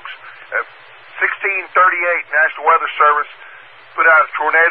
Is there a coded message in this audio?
Yes: Emergency Alert System (SAME) tones from 1.1 to 3.7 s, heard as harsh buzzing bursts, with speech.